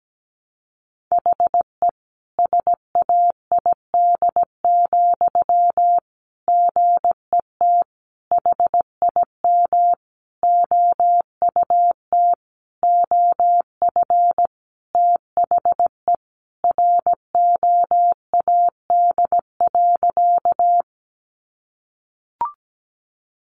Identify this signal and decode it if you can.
Morse code, heard as a keyed tone, about 17 wpm, 700 Hz. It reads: HE SAID, GET HIM OUT OF THE ROAD.
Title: Sentences from Top 400 Words 17wpm